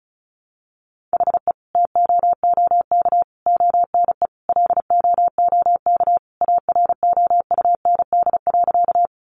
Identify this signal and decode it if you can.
Morse code, heard as a keyed tone, about 34 wpm, 700 Hz. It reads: HE TOOK ONE LOOK AROUND.
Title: Sentences from Top 600 Words 35wpm